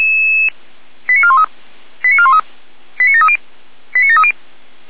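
Five-tone selective call (ZVEI), heard as quick runs of five tones, tones of about 70 ms. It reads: 89413 89413 89730 89730
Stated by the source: Received by SDR.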